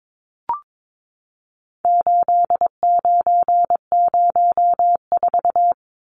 Morse code, heard as a keyed tone, about 22 wpm, 700 Hz. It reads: 8904
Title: Numbers - 4-Digit 22wpm